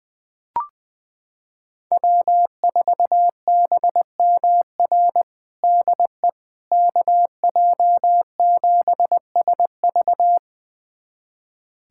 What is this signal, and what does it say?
Morse code, heard as a keyed tone, about 21 wpm, 700 Hz. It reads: W4BMR DE KJ7SV
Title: USA Callsign DE Callsign - 5 Characters - 20wpm